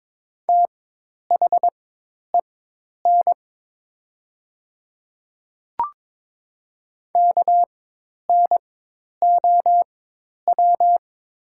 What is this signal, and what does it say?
Morse code, heard as a keyed tone, about 22 wpm, spaced as for 10 wpm, 700 Hz. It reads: THEN KNOW
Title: 4x Character Spacing - Top 100 Words - 22wpm